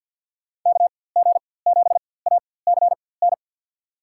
Morse code, heard as a keyed tone, about 47 wpm, spaced as for 23 wpm, 700 Hz. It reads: KG7ACN
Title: USA Call Signs - 6 Characters - 4x Character Spacing - 50wpm